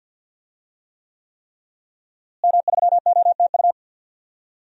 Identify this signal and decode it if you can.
Morse code, heard as a keyed tone, about 46 wpm, 700 Hz. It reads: M3OTU
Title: DXCC-100 Call Signs - 5 Characters - Repeated 3x - 50wpm